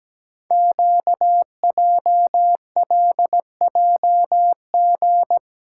Morse code, heard as a keyed tone, about 17 wpm, 700 Hz. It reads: QJLJG
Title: Code Groups - 5 Letters - 17wpm